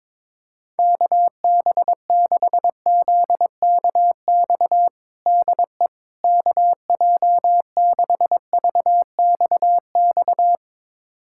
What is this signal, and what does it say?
Morse code, heard as a keyed tone, about 22 wpm, 700 Hz. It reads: KB6ZKX DE KJ6VXX
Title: USA Callsign DE Callsign - 6 Characters - 22wpm